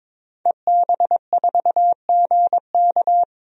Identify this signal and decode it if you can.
Morse code, heard as a keyed tone, about 22 wpm, 700 Hz. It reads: EB4GK
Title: DXCC-100 Call Signs - 5 Characters - Repeated 2x - 22wpm